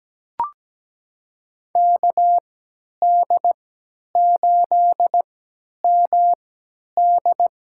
Morse code, heard as a keyed tone, about 17 wpm, spaced as for 10 wpm, 700 Hz. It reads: KD8MD
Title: USA Call Signs - 3x Character Spacing - 17wpm